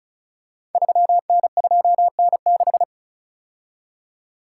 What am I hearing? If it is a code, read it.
Morse code, heard as a keyed tone, about 36 wpm, 690 Hz. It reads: 3N2N6
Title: Code Groups - 5 Letter:Numbers - 35wpm